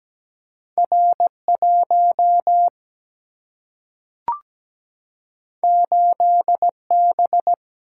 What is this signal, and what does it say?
Morse code, heard as a keyed tone, about 17 wpm, 700 Hz. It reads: R 1 8 B
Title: Single-Letter Single-Number 17wpm